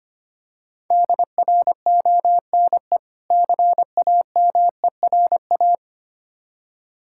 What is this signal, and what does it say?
Morse code, heard as a keyed tone, about 25 wpm, 690 Hz. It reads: DRONE CAMERA
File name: practice / 25wpm / Difficult Words with Hints - 25wpm